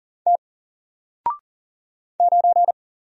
Morse code, heard as a keyed tone, about 40 wpm, 700 Hz. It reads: T 9
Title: Single Letter-Number Alt 40wpm